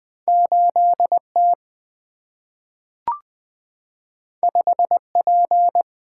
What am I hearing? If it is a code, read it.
Morse code, heard as a keyed tone, about 20 wpm, 690 Hz. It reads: 8T 5P
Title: Two Letter-Number Combinations - 20wpm